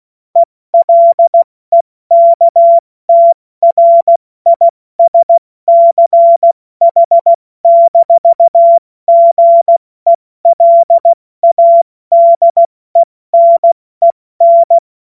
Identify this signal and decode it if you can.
Morse code, heard as a keyed tone, about 16 wpm, spaced as for 14 wpm, 680 Hz. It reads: ELEKTRISCH-GELADENEN
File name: Morsekurs / Witze / Witz 17 Juni Mai 2025 70 F80